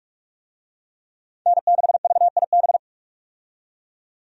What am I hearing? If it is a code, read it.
Morse code, heard as a keyed tone, about 41 wpm, 700 Hz. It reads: N6VIB
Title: USA Call Signs - 5 Characters - 45wpm